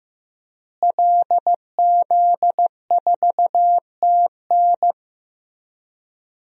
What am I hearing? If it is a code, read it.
Morse code, heard as a keyed tone, about 15 wpm, 700 Hz. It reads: LZ4TN
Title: DXCC-100 Call Signs - 5 Characters - Repeated 3x - 15wpm